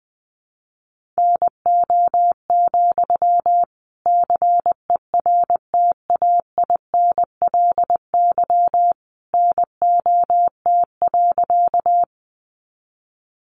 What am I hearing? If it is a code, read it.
Morse code, heard as a keyed tone, about 20 wpm, 700 Hz. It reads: NO, CERTAINLY NOT.
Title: Sentences from Top 900 Words 20wpm